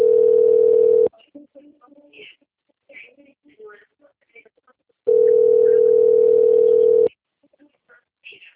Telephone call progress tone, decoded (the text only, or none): ringback tone